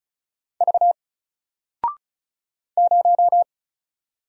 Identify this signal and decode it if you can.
Morse code, heard as a keyed tone, about 36 wpm, 700 Hz. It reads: V 0
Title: Single Letter-Number 35wpm